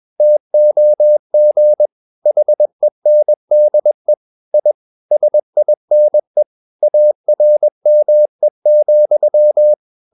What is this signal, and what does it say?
Morse code, heard as a keyed tone, about 21 wpm, 600 Hz. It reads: TOG HENDE I SINE ARME,
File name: Ferie QRS 21wpm